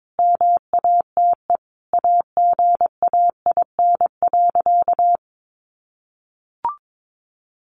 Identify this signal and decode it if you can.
Morse code, heard as a keyed tone, about 22 wpm, 700 Hz. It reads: MATE AGAIN.
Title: A Sound Of Thunder 22wpm